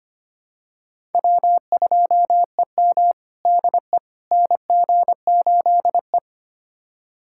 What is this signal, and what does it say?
Morse code, heard as a keyed tone, about 25 wpm, 700 Hz. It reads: W2EM DE NG8E
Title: USA Callsign DE Callsign - 4 Characters - 25wpm